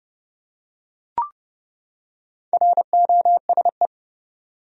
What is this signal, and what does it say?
Morse code, heard as a keyed tone, about 30 wpm, 700 Hz. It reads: ROSE
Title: USA Common Names - Max 4 Letters - 30wpm